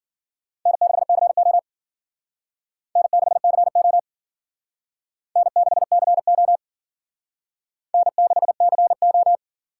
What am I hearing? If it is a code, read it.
Morse code, heard as a keyed tone, about 50 wpm, 700 Hz. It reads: N6CO N6CO N6CO N6CO
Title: Speed Racing - USA Call Signs 40wpm